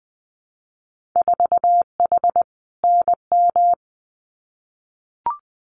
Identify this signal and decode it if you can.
Morse code, heard as a keyed tone, about 20 wpm, 700 Hz. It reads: 4H NM
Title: Winter Field Day Exchanges Encore 20wpm